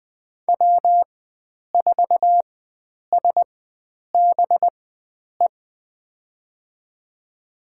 Morse code, heard as a keyed tone, about 20 wpm, spaced as for 9 wpm, 700 Hz. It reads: W4SBE
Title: USA Call Signs - 5 Characters - 4x Character Spacing - 20wpm